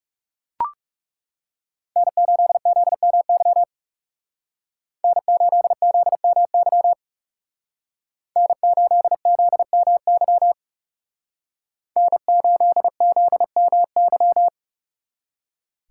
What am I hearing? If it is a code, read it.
Morse code, heard as a keyed tone, about 38 wpm, 700 Hz. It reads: N8ZMY N8ZMY N8ZMY N8ZMY